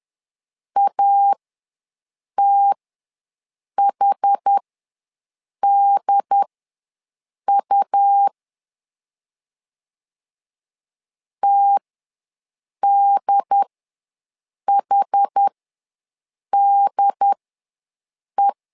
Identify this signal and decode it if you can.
Morse code, heard as a keyed tone, about 10 wpm, spaced as for 6 wpm, 800 Hz. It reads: ATHDU TDHDE
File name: LEZIONI / 2 UHDN / 2 UHD ws 10wpm 2058